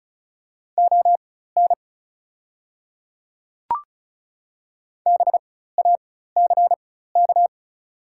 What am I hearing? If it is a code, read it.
Morse code, heard as a keyed tone, about 36 wpm, spaced as for 16 wpm, 700 Hz. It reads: ON BACK
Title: 4x Character Spacing - Top 300 Words - 35wpm